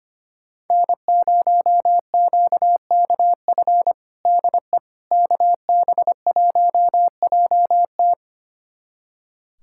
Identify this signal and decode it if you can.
Morse code, heard as a keyed tone, about 25 wpm, 700 Hz. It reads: N0QKF DE KB1JT
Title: USA Callsign DE Callsign - 5 Characters - 25wpm